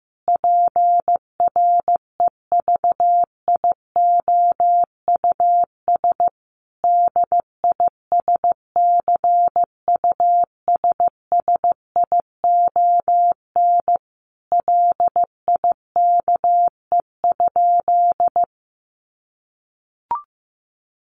Morse code, heard as a keyed tone, about 14 wpm, 700 Hz. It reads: PREVIOUS DISCUSSION LIKE?
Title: Sentences from Top 1500 Words 15wpm